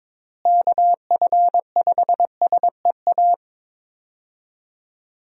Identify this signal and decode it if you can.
Morse code, heard as a keyed tone, about 22 wpm, 700 Hz. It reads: KF5SEA
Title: USA Call Signs - 6 Characters - Repeated 3x - 22wpm